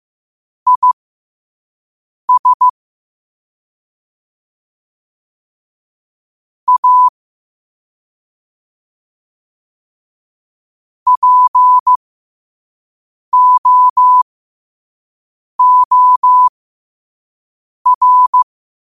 Morse code, heard as a keyed tone, about 15 wpm, spaced as for 5 wpm, 1000 Hz. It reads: IS A POOR
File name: qotdfiles / 5wpm